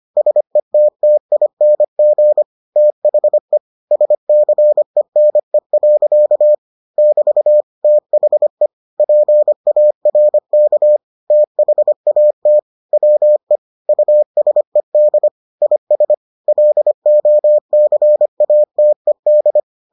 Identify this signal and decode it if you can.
Morse code, heard as a keyed tone, about 25 wpm, 600 Hz. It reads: SETTING THE SCENE. <BT> THE PARK THAT WE USED IS LOCATED